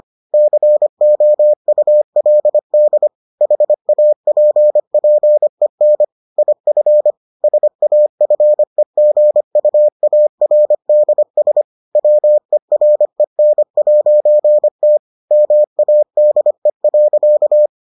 Morse code, heard as a keyed tone, about 25 wpm, 600 Hz. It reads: COULD HAPPEN IF SAFEGUARDS WEREN'T MADE.